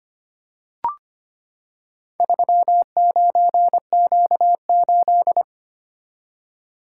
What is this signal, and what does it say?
Morse code, heard as a keyed tone, about 25 wpm, 700 Hz. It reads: 39Q8
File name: practice / 25wpm / Four Letter-Number Combinations - 25wpm